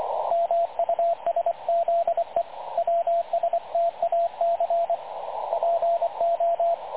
Morse code, heard as a keyed tone, about 25 wpm, 680 Hz, with radio noise. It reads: MUSZE WSTAC PO
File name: dlazaawansowanych / nagraniazpasma / websdr recording 3567.0kHz